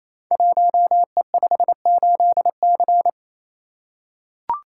Morse code, heard as a keyed tone, about 27 wpm, 700 Hz. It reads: 1E58C
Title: Code Groups - 5 Letter:Numbers - 28wpm